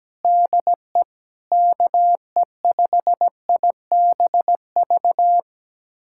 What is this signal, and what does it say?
Morse code, heard as a keyed tone, about 17 wpm, 700 Hz. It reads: DE KE5IBV
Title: USA Callsign DE Callsign 17wpm